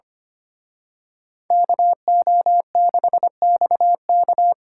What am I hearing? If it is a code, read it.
Morse code, heard as a keyed tone, about 29 wpm, 700 Hz. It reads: KO6XK
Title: Speed Racing - USA 2x2 Call Signs 25wpm